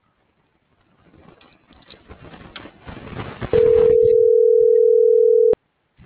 Ringback tone (North America), 440 + 480 Hz.